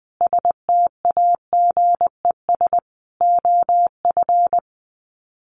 Morse code, heard as a keyed tone, about 20 wpm, 700 Hz. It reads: STAGES OF
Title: Sets of 5 Words Encore 20wpm